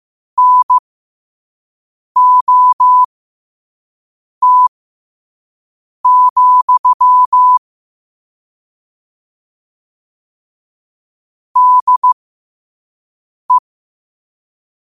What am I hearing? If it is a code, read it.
Morse code, heard as a keyed tone, about 15 wpm, spaced as for 5 wpm, 1000 Hz. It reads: NOT, DE
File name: qotdfiles / 5wpm